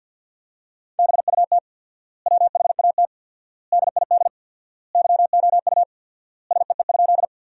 Morse code, heard as a keyed tone, about 52 wpm, 700 Hz. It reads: BUT WHAT DID YOU SEE?
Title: 4x Word-Spacing - Sentences from Top 100 Words - 50wpm